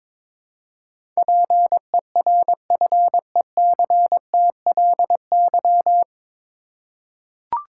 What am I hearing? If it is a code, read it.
Morse code, heard as a keyed tone, about 22 wpm, 700 Hz. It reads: PERFECTLY
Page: Morse Code Ninja practice 22wpm